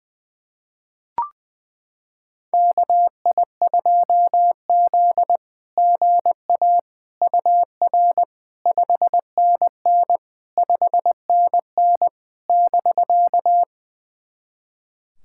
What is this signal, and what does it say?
Morse code, heard as a keyed tone, about 21 wpm, 700 Hz. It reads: KI2Z GA UR 5NN 5NN <BK>
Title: SOTA QSO Practice 20wpm